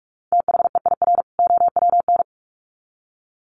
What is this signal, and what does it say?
Morse code, heard as a keyed tone, about 44 wpm, 700 Hz. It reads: THEIR OWN
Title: Sets of 2 Words 45wpm